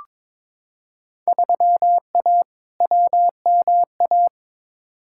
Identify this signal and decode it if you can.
Morse code, heard as a keyed tone, about 22 wpm, 700 Hz. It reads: 3A WMA
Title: Field Day Exchanges - 22wpm